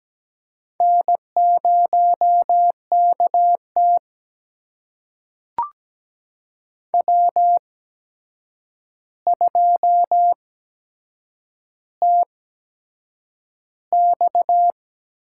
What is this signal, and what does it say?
Morse code, heard as a keyed tone, about 17 wpm, spaced as for 7 wpm, 700 Hz. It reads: N0KT W2TX
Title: USA Call Signs - 4 Characters - 8x Character Spacing - 17wpm